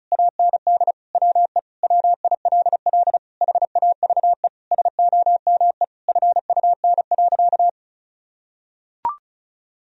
Morse code, heard as a keyed tone, about 34 wpm, 700 Hz. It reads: AND WE WILL HAVE SOME FUN.
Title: Sentences from Top 1000 Words - Review Part 4 - 35wpm